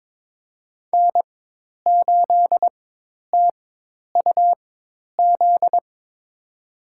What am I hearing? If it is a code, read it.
Morse code, heard as a keyed tone, about 21 wpm, spaced as for 10 wpm, 700 Hz. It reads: N8TUZ
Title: USA Call Signs - 4x Character Spacing - 22wpm